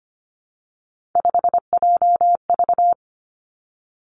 Morse code, heard as a keyed tone, about 25 wpm, 700 Hz. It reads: 5JV
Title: Three Challenging Letter-Number Combinations 25wpm